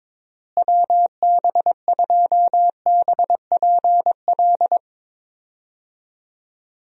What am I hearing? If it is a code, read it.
Morse code, heard as a keyed tone, about 22 wpm, 700 Hz. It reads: WB2BPL
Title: USA Call Signs - 6 Characters - Repeated 3x - 22wpm